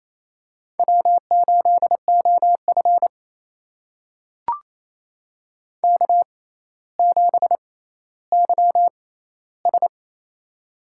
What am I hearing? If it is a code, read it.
Morse code, heard as a keyed tone, about 28 wpm, spaced as for 14 wpm, 700 Hz. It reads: W8OF K7YS
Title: USA Call Signs - 4 Characters - 6x Character Spacing - 28wpm